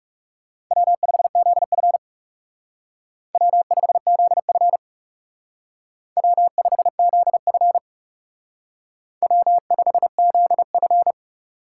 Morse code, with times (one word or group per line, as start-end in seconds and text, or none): 0.7-2.0 W5ZF
3.3-4.8 W5ZF
6.2-7.8 W5ZF
9.2-11.1 W5ZF